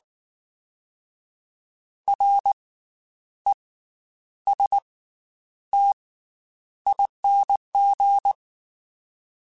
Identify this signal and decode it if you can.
Morse code, heard as a keyed tone, about 19 wpm, 800 Hz. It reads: R E S T ING
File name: Special INGs / Special INGs-mp3s / Slow ing @18